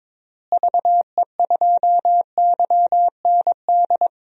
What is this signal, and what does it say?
Morse code, heard as a keyed tone, about 22 wpm, 700 Hz. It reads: VE2YND
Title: DXCC-100 Call Signs - 6 Characters - Repeated 3x - 22wpm